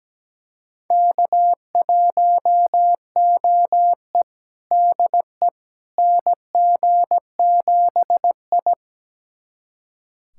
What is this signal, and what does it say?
Morse code, heard as a keyed tone, about 17 wpm, 700 Hz. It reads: K1OE DE NG7I
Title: USA Callsign DE Callsign - 4 Characters - 17wpm